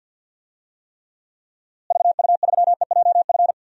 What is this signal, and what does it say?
Morse code, heard as a keyed tone, about 52 wpm, 700 Hz. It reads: VU3EJF
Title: DXCC-100 Call Signs - 6 Characters - Repeated 2x - 50wpm